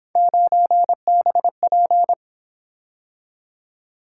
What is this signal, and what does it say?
Morse code, heard as a keyed tone, about 28 wpm, 700 Hz. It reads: 9BP
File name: Speed-Racing - Three Letter-Number Combinations - 22wpm